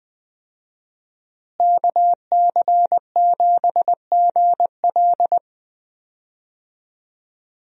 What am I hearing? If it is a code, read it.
Morse code, heard as a keyed tone, about 21 wpm, 700 Hz. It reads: KC7GL
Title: USA Call Signs - 5 Characters - Repeated 2x - 20wpm